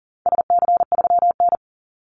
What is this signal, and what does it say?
Morse code, heard as a keyed tone, about 40 wpm, 700 Hz. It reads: SC3N
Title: Call Signs - 40wpm